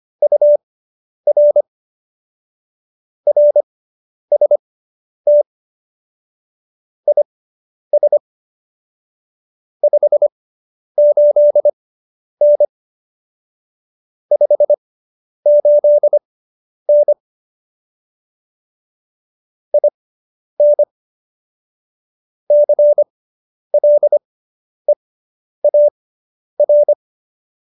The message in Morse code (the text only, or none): UR RST IS 58N 58N IN CLEAR